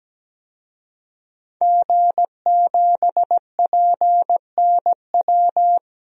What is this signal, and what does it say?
Morse code, heard as a keyed tone, about 17 wpm, 700 Hz. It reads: G7PNW
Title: DXCC-100 Call Signs - 5 Characters - 17wpm